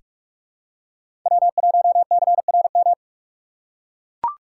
Morse code, heard as a keyed tone, about 43 wpm, 700 Hz. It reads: W1CRM
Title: USA Call Signs - 5 Characters - 4x Character Spacing - 45wpm